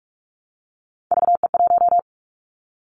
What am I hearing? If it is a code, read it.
Morse code, heard as a keyed tone, about 44 wpm, 700 Hz. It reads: VE1
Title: Callsign Prefixes for Top 25 DXCC - Repeated 3x - 45wpm